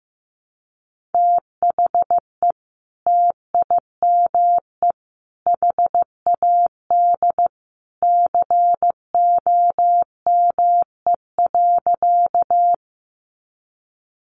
Morse code, heard as a keyed tone, about 15 wpm, 700 Hz. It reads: THE TIME HAD COME.
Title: Sentences from Top 100 Words 15wpm